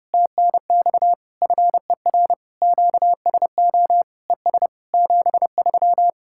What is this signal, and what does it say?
Morse code, heard as a keyed tone, about 34 wpm, 700 Hz. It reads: TNX FER QSO ES 73
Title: Speed-Racing Common QSO Phrases - 30wpm